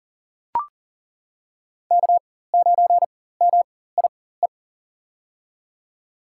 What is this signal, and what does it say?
Morse code, heard as a keyed tone, about 39 wpm, spaced as for 19 wpm, 700 Hz. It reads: K9MIE
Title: USA Call Signs - 4x Character Spacing - 40wpm